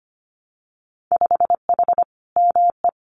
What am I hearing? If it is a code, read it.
Morse code, heard as a keyed tone, about 25 wpm, 700 Hz. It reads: 5H ME
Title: Winter Field Day Exchanges Encore 25wpm